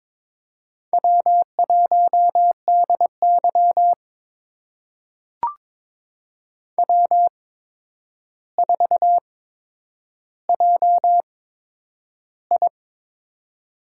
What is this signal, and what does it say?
Morse code, heard as a keyed tone, about 22 wpm, spaced as for 9 wpm, 700 Hz. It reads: W1DY W4JI
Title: USA Call Signs - 4 Characters - 8x Character Spacing - 22wpm